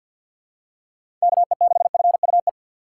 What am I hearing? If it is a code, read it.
Morse code, heard as a keyed tone, about 54 wpm, 700 Hz. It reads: KE6FUE
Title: USA Call Signs - 6 Characters - 6x Character Spacing - 50wpm